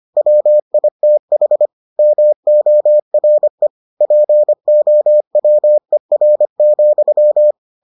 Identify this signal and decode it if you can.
Morse code, heard as a keyed tone, about 25 wpm, 600 Hz. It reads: WITH MORE POWER,